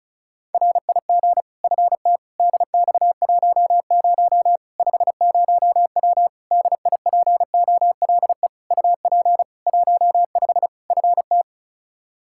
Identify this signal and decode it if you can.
Morse code, heard as a keyed tone, about 35 wpm, 700 Hz. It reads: RIG FT DX10 50W DIPOLE UP 15 FT
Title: K1USN SST QSO Practice 35wpm